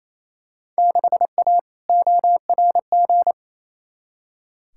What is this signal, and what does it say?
Morse code, heard as a keyed tone, about 28 wpm, 700 Hz. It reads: 6A ORG
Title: Field Day Exchanges Encore - 28wpm